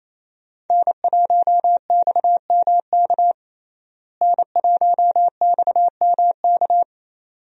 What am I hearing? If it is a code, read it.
Morse code, heard as a keyed tone, about 28 wpm, 700 Hz. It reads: N1XMK N1XMK
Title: USA Call Signs - 5 Characters - Repeated 2x - 28wpm